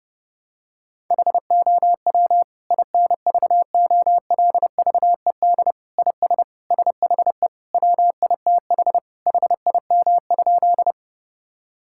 Morse code, heard as a keyed tone, about 32 wpm, 700 Hz. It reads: HOW INVOLVED IS SHE WITH HIM?
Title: Sentences from Top 1000 Words - Review Part 5 - 30wpm